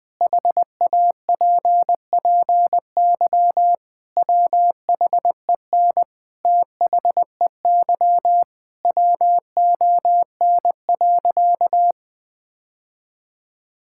Morse code, heard as a keyed tone, about 20 wpm, 700 Hz. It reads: HAPPY WHEN THEY WON.
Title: Sentences with 2-Word Phrases - 20wpm